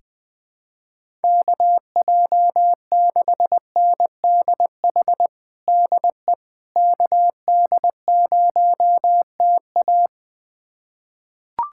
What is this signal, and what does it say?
Morse code, heard as a keyed tone, about 20 wpm, 700 Hz. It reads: KJ6NDH DE KD0TA
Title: USA Callsign DE Callsign 20wpm